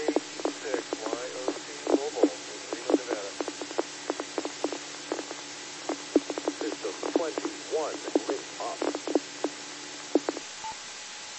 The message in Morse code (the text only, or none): none